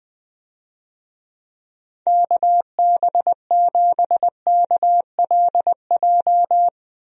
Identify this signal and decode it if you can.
Morse code, heard as a keyed tone, about 19 wpm, 700 Hz. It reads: KB7KLJ
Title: USA Call Signs - 6 Characters - Repeated 2x - 20wpm